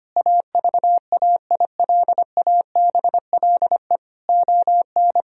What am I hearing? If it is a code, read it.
Morse code, heard as a keyed tone, about 25 wpm, 700 Hz. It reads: AVAILABLE ON
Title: Sentences from Top 1000 Words - Review Part 3 - 25wpm